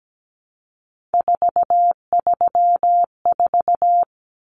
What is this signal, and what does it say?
Morse code, heard as a keyed tone, about 17 wpm, 700 Hz. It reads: 434
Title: Common Cut Numbers - 3-Digit 17wpm